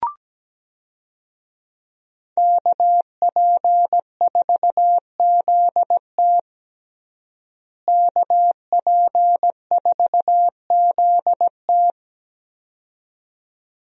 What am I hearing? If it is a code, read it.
Morse code, heard as a keyed tone, about 17 wpm, 700 Hz. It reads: KP4ZT KP4ZT
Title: USA Call Signs - 5 Characters - Repeated 2x - 17wpm